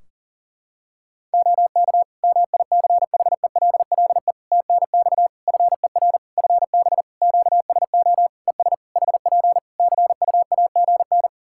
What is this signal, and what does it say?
Morse code, heard as a keyed tone, about 40 wpm, 700 Hz. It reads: OK MICHELLE TNX FER FB QSO ES HP CUAGN
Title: Practice QSO - Exchange 4 - Ending - 40wpm